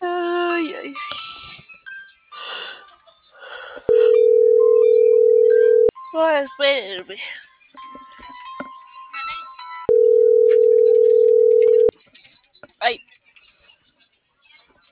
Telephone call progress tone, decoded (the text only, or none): ringback tone